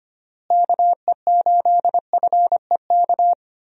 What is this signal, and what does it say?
Morse code, heard as a keyed tone, about 25 wpm, 700 Hz. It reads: KE8FEK
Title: USA Call Signs - 6 Characters - Repeated 3x - 25wpm